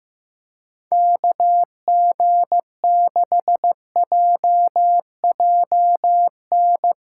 Morse code, heard as a keyed tone, about 15 wpm, 700 Hz. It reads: KG6JJN
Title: USA Callsign DE Callsign - 6 Characters - 15wpm